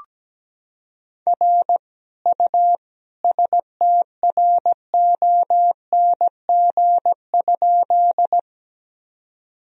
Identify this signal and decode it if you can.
Morse code, heard as a keyed tone, about 17 wpm, 700 Hz. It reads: R U STRONG?
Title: Sentences from Top 500 Words and Abbreviations 17wpm